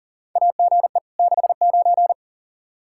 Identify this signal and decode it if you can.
Morse code, heard as a keyed tone, about 40 wpm, 700 Hz. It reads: AGE 69
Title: Practice QSO - Exchange 3 - More Info - 40wpm